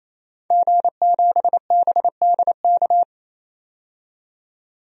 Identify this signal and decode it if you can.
Morse code, heard as a keyed tone, about 28 wpm, 700 Hz. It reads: G7BDK
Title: DXCC-100 Call Signs - 5 Characters - 28wpm